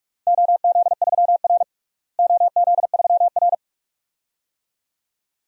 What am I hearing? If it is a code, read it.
Morse code, heard as a keyed tone, about 41 wpm, 700 Hz. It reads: OZ3R OZ3R
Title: DXCC-100 Call Signs - 4 Characters - Repeated 2x - 45wpm